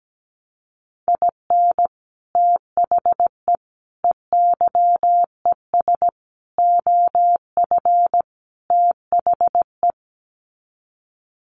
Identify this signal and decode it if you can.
Morse code, heard as a keyed tone, about 17 wpm, 700 Hz. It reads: IN THE EYES OF THE
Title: Sets of 5 Words 17wpm